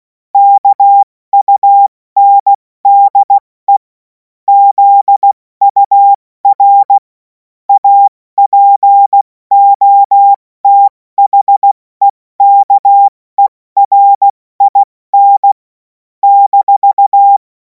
Morse code, heard as a keyed tone, about 16 wpm, spaced as for 14 wpm, 800 Hz. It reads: KUNDE ZUR APOTHEKERIN -